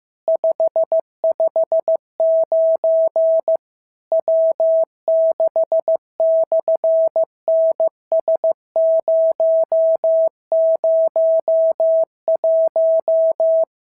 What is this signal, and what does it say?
Morse code, heard as a keyed tone, about 15 wpm, 650 Hz. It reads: 559 W6/NS001
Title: SOTA QSO Practice 15wpm